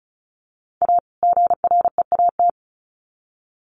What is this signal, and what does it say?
Morse code, heard as a keyed tone, about 34 wpm, 700 Hz. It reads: A GREAT
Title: Sets of 2 Words 35wpm